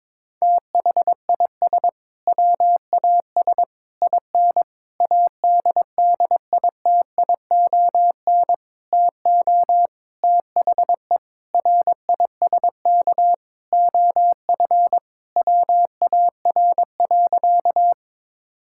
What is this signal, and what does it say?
Morse code, heard as a keyed tone, about 22 wpm, 700 Hz. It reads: THIS WAS IN ADDITION TO THE RISK OF WAR.
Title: Sentences from Top 1000 Words - Review Part 6 - 22wpm